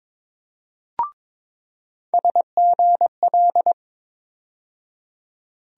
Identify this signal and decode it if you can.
Morse code, heard as a keyed tone, about 22 wpm, 700 Hz. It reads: SGL
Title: USA Call Sign Postfixes - 3 Letters 22wpm